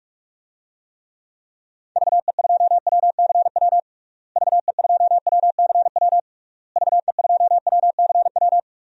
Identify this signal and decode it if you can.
Morse code, heard as a keyed tone, about 41 wpm, 700 Hz. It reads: VE2WCW VE2WCW VE2WCW
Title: DXCC-100 Call Signs - 6 Characters - Repeated 3x - 45wpm